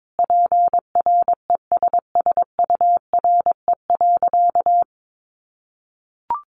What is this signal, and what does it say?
Morse code, heard as a keyed tone, about 22 wpm, 700 Hz. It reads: PRESSURE.
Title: Sentences from Top 1400 Words 22wpm